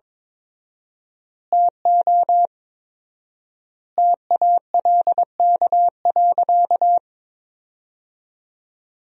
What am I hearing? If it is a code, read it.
Morse code, heard as a keyed tone, about 22 wpm, 700 Hz. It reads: TO TALK.
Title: 4x Word-Spacing - Sentences from Top 400 Words - 22wpm